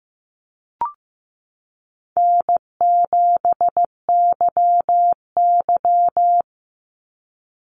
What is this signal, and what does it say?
Morse code, heard as a keyed tone, about 15 wpm, 700 Hz. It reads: N7YY
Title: NA SOTA Chaser Callsigns 15wpm